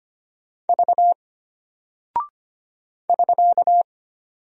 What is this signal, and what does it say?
Morse code, heard as a keyed tone, about 25 wpm, 700 Hz. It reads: V <SK>
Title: Single Character (Letter:Number:Punctuation:Prosign) Alt 25wpm